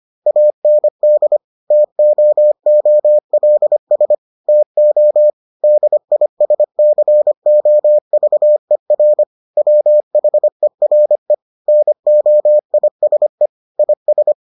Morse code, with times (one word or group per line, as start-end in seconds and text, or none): 0.3-1.4 AND
1.7-4.1 TOOLS
4.5-5.3 TO
5.6-9.2 DISCOVER
9.6-11.3 WHERE
11.7-13.5 NOISE
13.8-14.3 IS